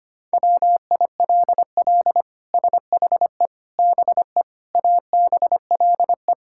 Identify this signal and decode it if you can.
Morse code, heard as a keyed tone, about 25 wpm, 700 Hz. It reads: WILL SHE BE ABLE
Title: Sentences from Top 1000 Words - Review Part 5 - 25wpm